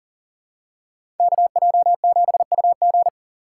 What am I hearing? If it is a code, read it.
Morse code, heard as a keyed tone, about 40 wpm, 700 Hz. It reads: KJ7UG
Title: USA Call Signs - 2x Character Spacing - 40wpm